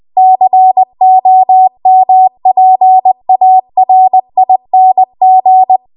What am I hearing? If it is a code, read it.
Morse code, heard as a keyed tone, about 20 wpm, 750 Hz. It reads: COMPARING
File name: Archive / 20 WPM / 20WPM